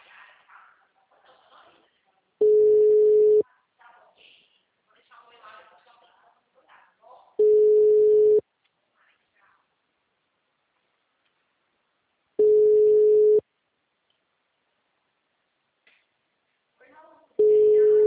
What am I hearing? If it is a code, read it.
Ringback tone (Europe, 425 Hz), with speech.